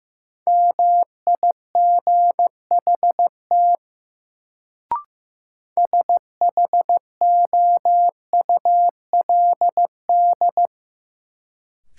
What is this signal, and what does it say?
Morse code, heard as a keyed tone, about 15 wpm, 700 Hz. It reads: MIGHT SHOULD
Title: Sentences from Top 300 Words - New Word Drill - 15wpm